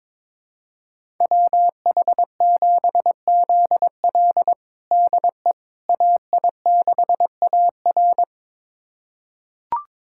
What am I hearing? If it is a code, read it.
Morse code, heard as a keyed tone, about 22 wpm, 700 Hz. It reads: WH7ZL DE AI6AR
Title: USA Callsign DE Callsign - 5 Characters - 22wpm